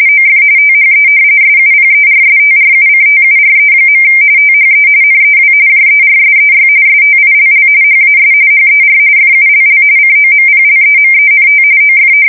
RTTY, about 45.45 baud. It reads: OF THE SUPERSTRUCTURE. THE NEW JERSEY STILL HAS HER FULL COMPLEMENT OF
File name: fsk170 / battleshipnj